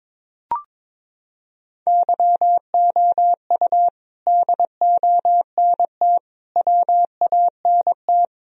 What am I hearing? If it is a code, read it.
Morse code, heard as a keyed tone, about 22 wpm, 700 Hz. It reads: YOU DONT WANT
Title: Sentences from Top 1000 Words - Review Part 1 - 22wpm